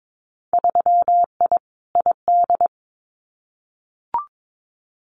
Morse code, heard as a keyed tone, about 22 wpm, 700 Hz. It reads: 3I ID
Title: Winter Field Day Exchanges 22wpm